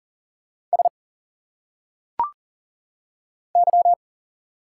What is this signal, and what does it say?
Morse code, heard as a keyed tone, about 40 wpm, 700 Hz. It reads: S Y
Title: Single Letters - Phonetic Pronunciation 40wpm